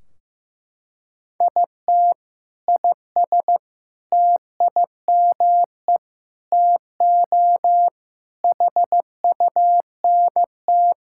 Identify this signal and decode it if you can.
Morse code, heard as a keyed tone, about 16 wpm, 700 Hz. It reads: IT IS TIME TO HUNT
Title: Sentences from Top 1600 Words 15wpm